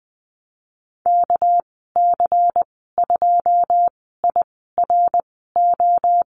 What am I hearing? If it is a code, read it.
Morse code, heard as a keyed tone, about 20 wpm, spaced as for 15 wpm, 700 Hz. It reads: KC2IRO
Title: 213 Lesson 34g - Number 8 - Call Signs 20wpm